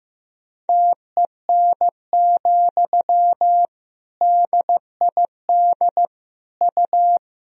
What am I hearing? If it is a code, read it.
Morse code, heard as a keyed tone, about 15 wpm, 700 Hz. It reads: TEN, DID U
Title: Sentences from Top 400 Words and Abbreviations 15wpm